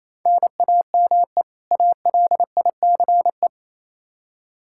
Morse code, heard as a keyed tone, about 28 wpm, 700 Hz. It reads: NAME ALICE
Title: Practice QSO - Exchange 1 - Beginning - 28wpm